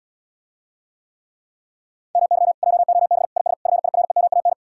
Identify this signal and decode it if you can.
Morse code, heard as a keyed tone, about 77 wpm, 700 Hz. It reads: MY JOB IS FIREMAN